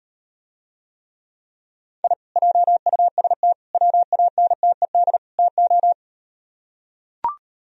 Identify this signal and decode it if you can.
Morse code, heard as a keyed tone, about 38 wpm, 700 Hz. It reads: I JUST WANTED TO